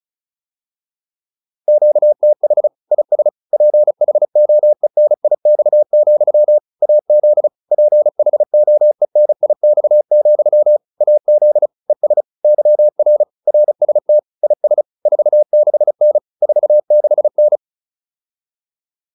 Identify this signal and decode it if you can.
Morse code, heard as a keyed tone, about 34 wpm, 600 Hz. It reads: QTH IS PHOENIX, AZ PHOENIX, AZ ES YR RST IS 46N 46N